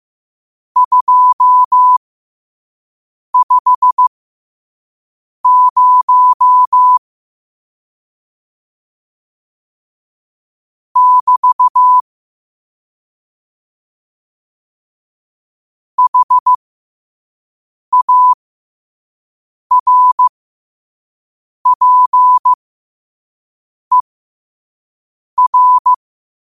Morse code, heard as a keyed tone, about 15 wpm, spaced as for 5 wpm, 1000 Hz. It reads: 250 <BT> HARPER
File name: 5wpm